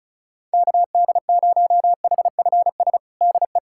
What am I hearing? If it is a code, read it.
Morse code, heard as a keyed tone, about 36 wpm, 700 Hz. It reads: KD0HFS DE
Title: USA Callsign DE Callsign - 6 Characters - 35wpm